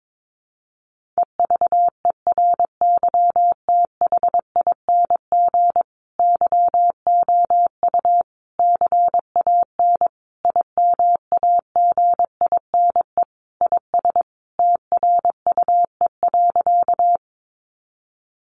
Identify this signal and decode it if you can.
Morse code, heard as a keyed tone, about 22 wpm, 700 Hz. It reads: EVERYTHING YOU CAN IMAGINE IS TRUE.
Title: Sentences from Top 1400 Words 22wpm